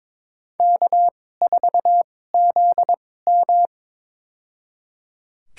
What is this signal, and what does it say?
Morse code, heard as a keyed tone, about 22 wpm, spaced as for 16 wpm, 700 Hz. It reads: K4ZM
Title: USA Call Signs - 2x Character Spacing - 22wpm